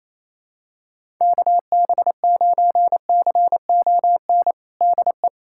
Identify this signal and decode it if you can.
Morse code, heard as a keyed tone, about 28 wpm, 700 Hz. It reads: KB9CON DE
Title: USA Callsign DE Callsign 28wpm